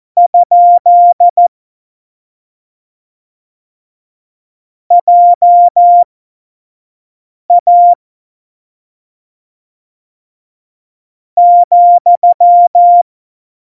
Morse code, heard as a keyed tone, about 14 wpm, spaced as for 5 wpm, 700 Hz. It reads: ? JA ,
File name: Morsekurs / Witze / Witz 20 March 2018, 25 F70